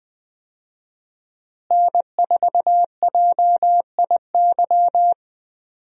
Morse code, heard as a keyed tone, about 21 wpm, 700 Hz. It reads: N4JIY
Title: USA Call Signs - 5 Characters - Repeated 3x - 20wpm